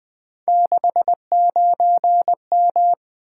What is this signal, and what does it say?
Morse code, heard as a keyed tone, about 20 wpm, 700 Hz. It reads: 69M
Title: Three Letter-Number Combinations - 20wpm